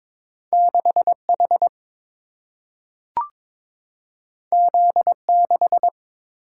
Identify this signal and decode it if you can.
Morse code, heard as a keyed tone, about 22 wpm, 700 Hz. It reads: 6H Z6
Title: Single-Letter Single-Number 22wpm